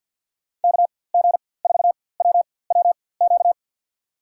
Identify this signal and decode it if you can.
Morse code, heard as a keyed tone, about 53 wpm, spaced as for 24 wpm, 700 Hz. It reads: KG4WWQ
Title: USA Call Signs - 6 Characters - 4x Character Spacing - 50wpm